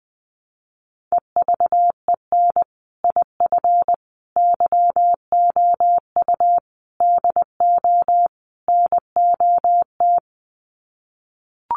Morse code, heard as a keyed tone, about 20 wpm, 700 Hz. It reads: EVEN IF YOU DO NOT
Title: Sets of 5 Words Encore 20wpm